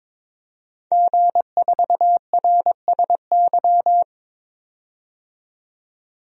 Morse code, heard as a keyed tone, about 22 wpm, 700 Hz. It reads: G4RSY
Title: DXCC-100 Call Signs - 5 Characters - 22wpm